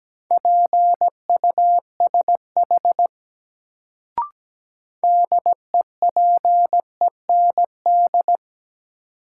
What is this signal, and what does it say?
Morse code, heard as a keyed tone, about 17 wpm, 700 Hz. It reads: PUSH DEPEND
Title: Sentences from Top 1300 Words - New Word Drill Alt - 17wpm